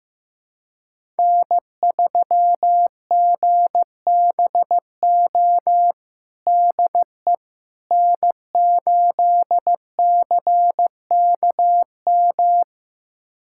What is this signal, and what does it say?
Morse code, heard as a keyed tone, about 15 wpm, 700 Hz. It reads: N3GBO DE N8CKM
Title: USA Callsign DE Callsign - 5 Characters - 15wpm